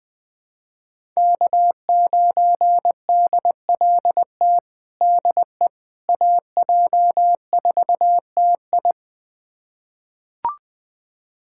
Morse code, heard as a keyed tone, about 20 wpm, 700 Hz. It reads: K9DLT DE AJ4TI
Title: USA Callsign DE Callsign - 5 Characters - 20wpm